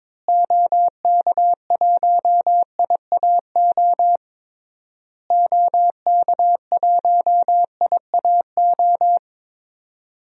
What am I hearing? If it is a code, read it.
Morse code, heard as a keyed tone, about 22 wpm, 700 Hz. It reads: OK1IAO OK1IAO
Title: DXCC-100 Call Signs - 6 Characters - Repeated 3x - 22wpm